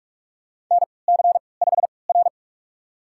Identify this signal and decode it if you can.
Morse code, heard as a keyed tone, about 43 wpm, spaced as for 25 wpm, 690 Hz. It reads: NC5R
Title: USA Call Signs - 4 Characters - 3x Character Spacing - 45wpm